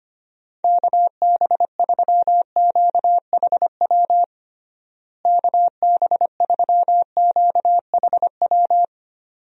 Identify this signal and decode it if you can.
Morse code, heard as a keyed tone, about 25 wpm, 700 Hz. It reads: KB3QHW KB3QHW
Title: USA Call Signs - 6 Characters - Repeated 2x - 25wpm